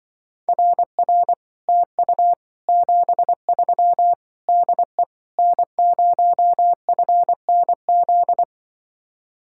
Morse code, heard as a keyed tone, about 24 wpm, 700 Hz. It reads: RR TU 73 DE N0FNZ
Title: Speed-Racing Common QSO Phrases - 20wpm